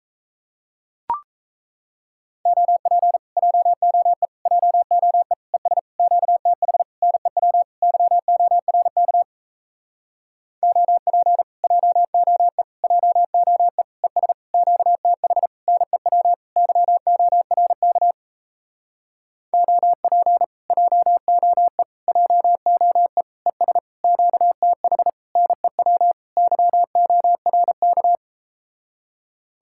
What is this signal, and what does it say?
Morse code, heard as a keyed tone, about 37 wpm, 700 Hz. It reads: OP JOE JOE ES QTH NEW YORK OP JOE JOE ES QTH NEW YORK OP JOE JOE ES QTH NEW YORK